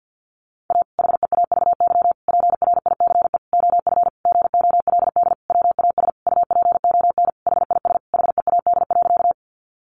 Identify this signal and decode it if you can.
Morse code, heard as a keyed tone, about 50 wpm, 700 Hz. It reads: A HEAVY PRICE OF GOLD WAS UPON HIS HEAD.
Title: Sentences from Top 1100 Words 50wpm